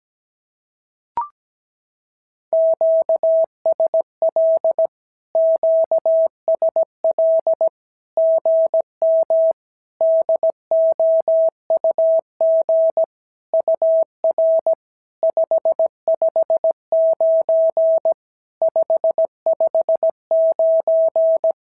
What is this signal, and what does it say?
Morse code, heard as a keyed tone, about 17 wpm, 650 Hz. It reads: QSL QSL GM DOUG UR 559 559
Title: SOTA QSO Practice 17wpm